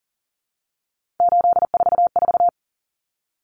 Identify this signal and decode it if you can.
Morse code, heard as a keyed tone, about 40 wpm, 700 Hz. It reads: 844
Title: Common Cut Numbers - 3-Digit 40wpm